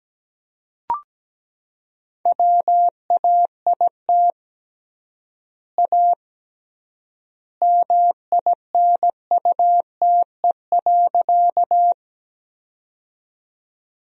Morse code, heard as a keyed tone, about 17 wpm, 700 Hz. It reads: WAIT A MINUTE.
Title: 3x Word-Spacing - Sentences from Top 500 Words - 17wpm